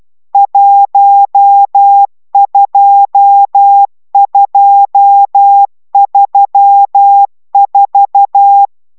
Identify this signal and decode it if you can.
Morse code, heard as a keyed tone, about 12 wpm, 800 Hz. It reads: 12234